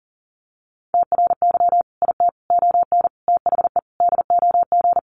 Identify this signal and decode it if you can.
Morse code, heard as a keyed tone, about 40 wpm, 700 Hz. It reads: TRY IT ON THE DOG
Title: Sentences from Top 400 Words 40wpm